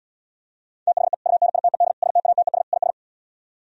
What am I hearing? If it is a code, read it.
Morse code, heard as a keyed tone, about 77 wpm, 700 Hz. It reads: THE UNITED STATES IS